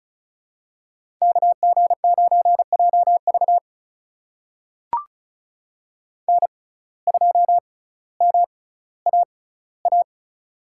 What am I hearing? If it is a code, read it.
Morse code, heard as a keyed tone, about 36 wpm, spaced as for 18 wpm, 700 Hz. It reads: KG9JV N2MAA
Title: USA Call Signs - 6x Character Spacing - 35wpm